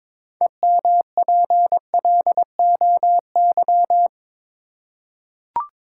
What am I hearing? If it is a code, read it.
Morse code, heard as a keyed tone, about 22 wpm, 700 Hz. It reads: EMPLOY
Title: Six Letter Words - 22wpm